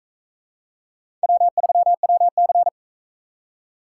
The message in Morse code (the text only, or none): W3WC